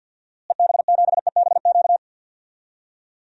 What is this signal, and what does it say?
Morse code, heard as a keyed tone, about 52 wpm, 700 Hz. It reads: EB7EBQ
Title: DXCC-100 Call Signs - 6 Characters - Repeated 3x - 50wpm